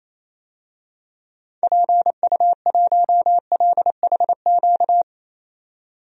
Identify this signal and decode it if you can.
Morse code, heard as a keyed tone, about 28 wpm, 700 Hz. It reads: PU1LHQ
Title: DXCC-100 Call Signs - 6 Characters - 28wpm